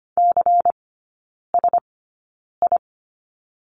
Morse code, heard as a keyed tone, about 25 wpm, spaced as for 9 wpm, 700 Hz. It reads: CSI
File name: course / 25:10wpm / 093 Lesson 16e - Letter c - Three Characters 25wpm